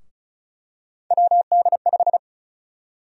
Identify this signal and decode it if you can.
Morse code, heard as a keyed tone, about 36 wpm, 700 Hz. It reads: WD5
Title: USA Call Sign Prefixes - Repeated 2x - 35wpm